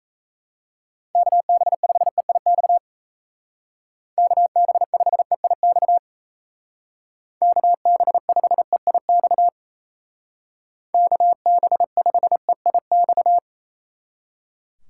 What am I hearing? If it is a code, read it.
Morse code, heard as a keyed tone, about 33 wpm, 700 Hz. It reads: KB5EIX KB5EIX KB5EIX KB5EIX
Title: Speed Racing - USA 2x3 Call Signs 28wpm